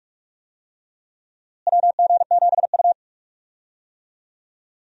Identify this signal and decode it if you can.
Morse code, heard as a keyed tone, about 46 wpm, 700 Hz. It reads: WG7U
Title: USA Call Signs - 4 Characters - Repeated 3x - 45wpm